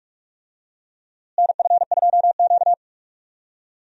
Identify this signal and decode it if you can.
Morse code, heard as a keyed tone, about 44 wpm, 700 Hz. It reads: NF2Q
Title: USA Call Signs - 4 Characters - Repeated 3x - 45wpm